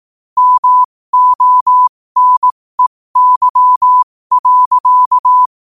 Morse code, heard as a keyed tone, about 18 wpm, spaced as for 15 wpm, 1000 Hz. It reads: MONEY.
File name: qotdfiles / 15wpm